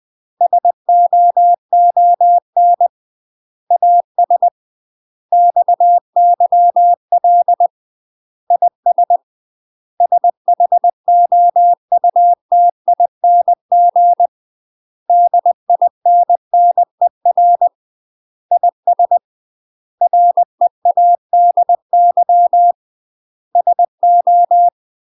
Morse code, heard as a keyed tone, about 20 wpm, 700 Hz. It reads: SOON AS XYL IS SHOUTING DINNER IS READY SO
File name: UK MASTER QSOs TRAINING / 20WPM / UK MASTER QSOs 0004